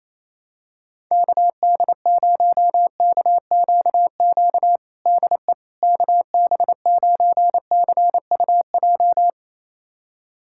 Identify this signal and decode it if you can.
Morse code, heard as a keyed tone, about 28 wpm, 700 Hz. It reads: KD0KQQ DE KB9CUJ
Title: USA Callsign DE Callsign - 6 Characters - 28wpm